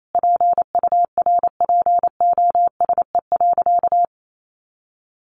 Morse code, heard as a keyed tone, about 28 wpm, 700 Hz. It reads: PURPOSE.
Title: Sentences from Top 800 Words 28wpm